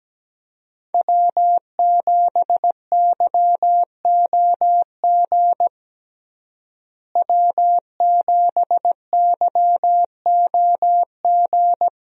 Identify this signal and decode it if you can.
Morse code, heard as a keyed tone, about 17 wpm, 700 Hz. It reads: W7YOG W7YOG
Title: USA Call Signs - 5 Characters - Repeated 3x - 17wpm